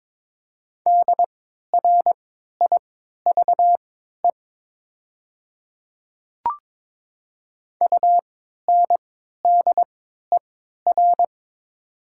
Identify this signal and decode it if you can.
Morse code, heard as a keyed tone, about 21 wpm, spaced as for 12 wpm, 700 Hz. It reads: DRIVE UNDER
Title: 3x Character Spacing - Top 500 Words - 22wpm